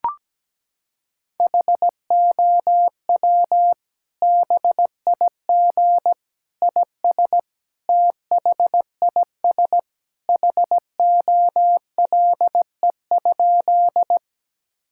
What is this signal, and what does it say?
Morse code, heard as a keyed tone, about 17 wpm, 700 Hz. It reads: HOW BIG IS THIS HOLE?
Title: Sentences from Top 1600 Words 17wpm